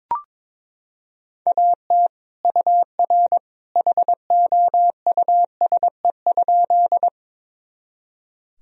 Morse code, heard as a keyed tone, about 22 wpm, 700 Hz. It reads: AT UR HOUSE?
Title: Sentences from Top 200 Words and Abbreviations 22wpm